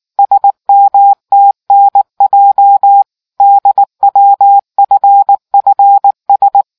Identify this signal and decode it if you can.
Morse code, heard as a keyed tone, about 19 wpm, 800 Hz. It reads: SMTNJ DWFFS